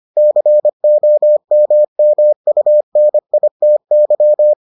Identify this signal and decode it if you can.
Morse code, heard as a keyed tone, about 25 wpm, 600 Hz. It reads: COMMUNITY